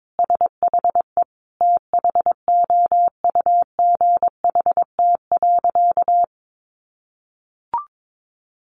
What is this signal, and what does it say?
Morse code, heard as a keyed tone, about 22 wpm, 700 Hz. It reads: SHE THOUGHT.
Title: Sentences from Top 800 Words 22wpm